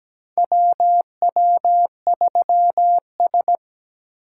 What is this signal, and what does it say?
Morse code, heard as a keyed tone, about 17 wpm, 700 Hz. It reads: WW3S
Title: USA Call Signs - 4 Characters - 6x Character Spacing - 17wpm